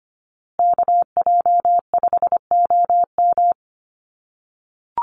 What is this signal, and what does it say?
Morse code, heard as a keyed tone, about 25 wpm, 700 Hz. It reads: KJ5OM
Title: USA Call Signs Repeated 2x 25wpm